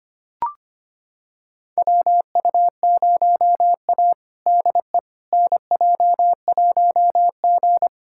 Morse code, heard as a keyed tone, about 25 wpm, 700 Hz. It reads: WU0A DE NJ1G
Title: USA Callsign DE Callsign - 4 Characters - 25wpm